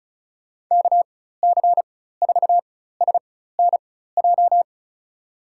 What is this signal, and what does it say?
Morse code, heard as a keyed tone, about 34 wpm, spaced as for 16 wpm, 700 Hz. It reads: KC4SNJ
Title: USA Call Signs - 6 Characters - 4x Character Spacing - 35wpm